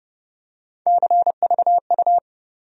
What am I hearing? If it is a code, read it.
Morse code, heard as a keyed tone, about 30 wpm, 700 Hz. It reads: CVU
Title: USA Call Sign Postfixes - 3 Letters 30wpm